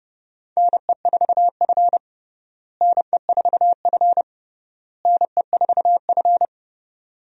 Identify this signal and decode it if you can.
Morse code, heard as a keyed tone, about 30 wpm, 700 Hz. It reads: NE4F NE4F NE4F
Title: USA Call Signs - 4 Characters - Repeated 3x - 30wpm